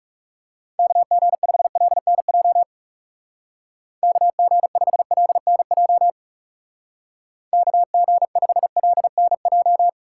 Morse code, heard as a keyed tone, about 35 wpm, 700 Hz. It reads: KG5LNJ KG5LNJ KG5LNJ
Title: Speed Racing - USA Call Signs 30wpm